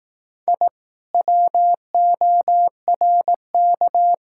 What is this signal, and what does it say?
Morse code, heard as a keyed tone, about 18 wpm, 700 Hz. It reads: I WORK
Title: Speed-Racing Common QSO Phrases Encore - 15wpm